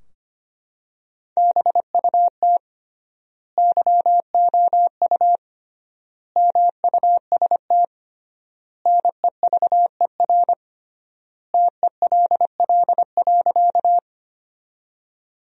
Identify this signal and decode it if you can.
Morse code, heard as a keyed tone, about 25 wpm, 700 Hz. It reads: BUT YOU MUST NEVER TELL.
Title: 3x Word-Spacing - Sentences from Top 300 Words - 25wpm